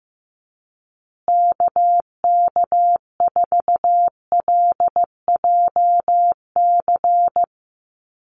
Morse code, heard as a keyed tone, about 15 wpm, 700 Hz. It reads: KK4LJC
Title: USA Call Signs Repeated 3x 15wpm